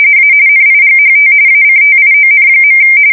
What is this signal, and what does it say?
RTTY, about 45.45 baud. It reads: A NIGHT ACTION.
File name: fsk170 / USS-New-Jersey